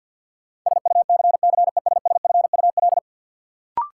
Morse code, heard as a keyed tone, about 52 wpm, 700 Hz. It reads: SUCCESSFUL
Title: Sentences from Top 900 Words - New Word Drill - 50wpm